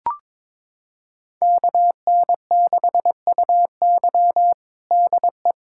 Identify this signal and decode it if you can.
Morse code, heard as a keyed tone, about 21 wpm, 700 Hz. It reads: KN6UY DE
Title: USA Callsign DE Callsign - 5 Characters - 22wpm